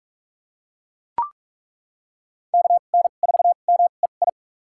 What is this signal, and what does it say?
Morse code, heard as a keyed tone, about 43 wpm, spaced as for 32 wpm, 700 Hz. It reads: KN4MEI
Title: USA Call Signs - 2x Character Spacing - 45wpm